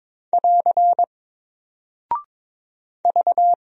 Morse code, heard as a keyed tone, about 22 wpm, 700 Hz. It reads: <AR> V
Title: Single Character (Letter:Number:Punctuation:Prosign) Alt 22wpm